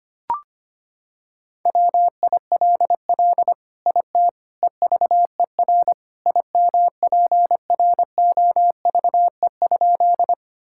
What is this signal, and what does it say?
Morse code, heard as a keyed tone, about 25 wpm, 700 Hz. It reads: WILL IT EVER IMPROVE?
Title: Sentences from Top 1000 Words - Review Part 5 - 25wpm